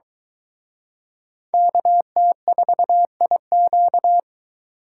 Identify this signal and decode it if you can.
Morse code, heard as a keyed tone, about 23 wpm, 700 Hz. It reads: KT4IQ